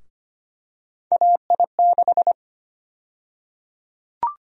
Morse code, heard as a keyed tone, about 25 wpm, 700 Hz. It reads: AI6
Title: USA Call Sign Prefixes - 25wpm